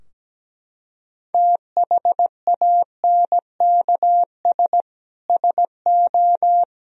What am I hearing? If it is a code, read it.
Morse code, heard as a keyed tone, about 17 wpm, 700 Hz. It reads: THANKS SO
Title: Sentences from Top 1000 Words - Review Part 3 - 17wpm